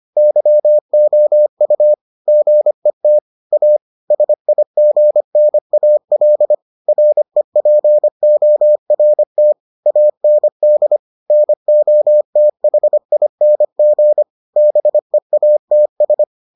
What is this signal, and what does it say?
Morse code, heard as a keyed tone, about 25 wpm, 600 Hz. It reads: YOU GET A SIGNAL REPORT AND NOTHING BEATS